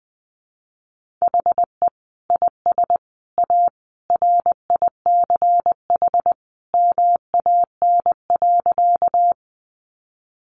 Morse code, heard as a keyed tone, about 20 wpm, 700 Hz. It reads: HE IS A RICH MAN.
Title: Sentences from Top 1100 Words 20wpm